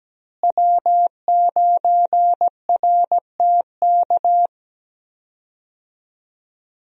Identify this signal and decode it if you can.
Morse code, heard as a keyed tone, about 17 wpm, 700 Hz. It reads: W9RTK
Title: USA Call Signs - 5 Characters - Repeated 2x - 17wpm